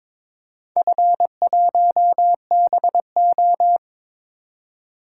Morse code, heard as a keyed tone, about 22 wpm, 700 Hz. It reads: F1BO
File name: practice / 22wpm / Four Letter-Number Combinations - 22wpm